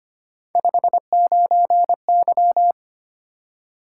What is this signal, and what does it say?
Morse code, heard as a keyed tone, about 25 wpm, 700 Hz. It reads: 59Y
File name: Three Letter-Number Combinations - 25wpm